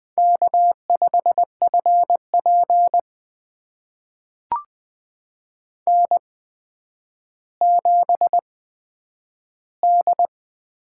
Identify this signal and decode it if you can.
Morse code, heard as a keyed tone, about 20 wpm, spaced as for 10 wpm, 700 Hz. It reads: K5FP N7D
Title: USA Call Signs - 8x Character Spacing - 20wpm